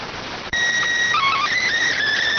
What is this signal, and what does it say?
SSTV (slow-scan television), PD 290.